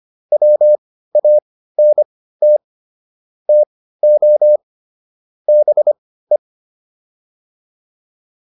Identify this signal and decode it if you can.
Morse code, heard as a keyed tone, about 25 wpm, spaced as for 15 wpm, 600 Hz. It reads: WANT TO BE